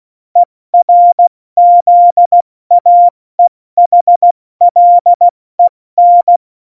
Morse code, heard as a keyed tone, about 16 wpm, 700 Hz. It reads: ERZAEHLEN
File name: Morsekurs / Witze / Witz 18 August 2020, 70 F80